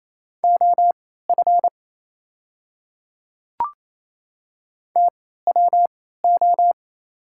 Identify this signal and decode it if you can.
Morse code, heard as a keyed tone, about 29 wpm, spaced as for 16 wpm, 690 Hz. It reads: OF TWO
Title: 3x Character Spacing - Top 100 Words - 28wpm